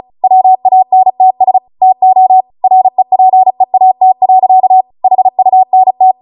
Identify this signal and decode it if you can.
Morse code, heard as a keyed tone, about 36 wpm, 750 Hz. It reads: WANTS TO REPEAT. HUNT
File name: Morse / 35WPM